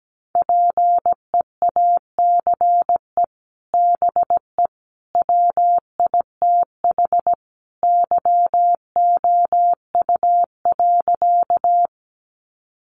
Morse code, heard as a keyed tone, about 17 wpm, 700 Hz. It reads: PEACE BE WITH YOU.
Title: Sentences from Top 1500 Words 17wpm